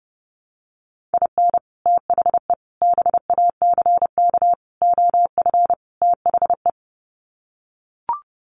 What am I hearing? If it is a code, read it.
Morse code, heard as a keyed tone, about 28 wpm, 700 Hz. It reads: IN THE BACK OF THE
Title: Sets of 5 Words 30wpm